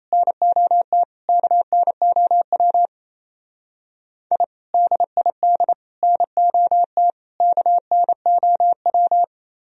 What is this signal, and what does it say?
Morse code, heard as a keyed tone, about 33 wpm, 700 Hz. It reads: NOT KNOW I DID NOT KNOW
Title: Speed-Racing - Sets of 4 Words 28wpm